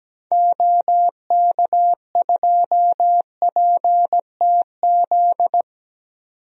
Morse code, heard as a keyed tone, about 17 wpm, 700 Hz. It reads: OK2PTZ
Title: DXCC-100 Call Signs - 6 Characters - 17wpm